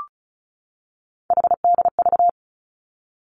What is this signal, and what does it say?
Morse code, heard as a keyed tone, about 36 wpm, 700 Hz. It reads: HDV
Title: Three Challenging Letter-Number Combinations 35wpm